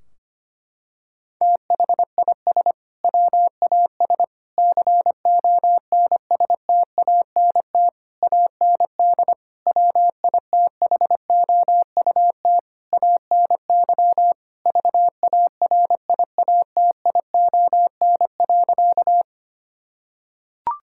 Morse code, heard as a keyed tone, about 25 wpm, 700 Hz. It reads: THIS WAS CONSTANT AND WITHOUT ANY VARIATION.
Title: Sentences from Top 1600 Words 25wpm